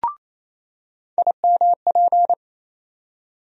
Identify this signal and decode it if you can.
Morse code, heard as a keyed tone, about 28 wpm, 700 Hz. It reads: IMP
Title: USA Call Sign Postfixes - 3 Letters 28wpm